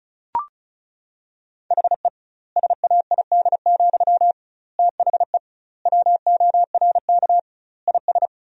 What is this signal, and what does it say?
Morse code, heard as a keyed tone, about 36 wpm, 700 Hz. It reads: HE SAID, THE WORK IS
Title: 2x Word-Spacing - Sentences from Top 400 Words - 35wpm